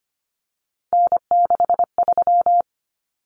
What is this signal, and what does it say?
Morse code, heard as a keyed tone, about 25 wpm, 700 Hz. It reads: N63
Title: Common Cut Numbers - 3-Digit 25wpm